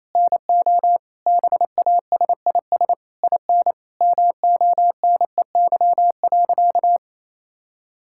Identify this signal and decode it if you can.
Morse code, heard as a keyed tone, about 27 wpm, 700 Hz. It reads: NO BASIS IN MONEY.
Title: Sentences from Top 1000 Words - Review Part 5 - 28wpm